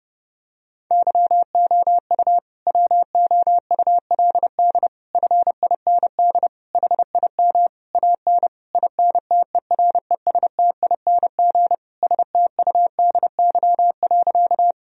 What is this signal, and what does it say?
Morse code, heard as a keyed tone, about 32 wpm, 700 Hz. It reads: YOU WOULD FIND HIM AN INTERESTING STUDY.
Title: Sentences from Top 1000 Words - Review Part 1 - 30wpm